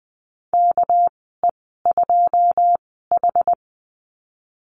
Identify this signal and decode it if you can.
Morse code, heard as a keyed tone, about 20 wpm, spaced as for 14 wpm, 700 Hz. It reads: KE2H
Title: 231 Lesson 37g - Letter x - Call Signs 20wpm